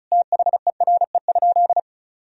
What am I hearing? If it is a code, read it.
Morse code, heard as a keyed tone, about 36 wpm, 700 Hz. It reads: THERE?
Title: Sentences from Top 1000 Words - Review Part 2 - 35wpm